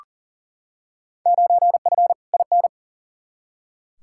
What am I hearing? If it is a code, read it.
Morse code, heard as a keyed tone, about 40 wpm, 700 Hz. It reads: 9F IN
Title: Field Day Exchanges Encore - 40wpm